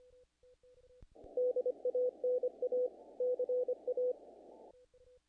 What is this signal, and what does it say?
Morse code, heard as a keyed tone, about 28 wpm, 500 Hz. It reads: DANA CA
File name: contests / naqp-cw-k5tr-op-k5pi- -1800